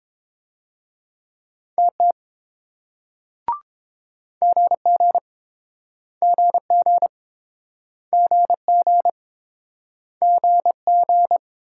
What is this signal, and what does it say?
Morse code, heard as a keyed tone, about 29 wpm, 700 Hz. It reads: TT GG GG GG GG